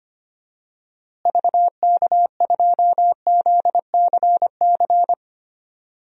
Morse code, heard as a keyed tone, about 25 wpm, 700 Hz. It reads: VK2ZCC
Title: DXCC-100 Call Signs - 6 Characters - Repeated 2x - 25wpm